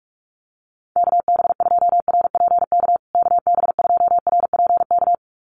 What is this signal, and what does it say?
Morse code, heard as a keyed tone, about 44 wpm, 700 Hz. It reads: KB2RPK KB2RPK
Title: USA Call Signs Repeated 2x 45wpm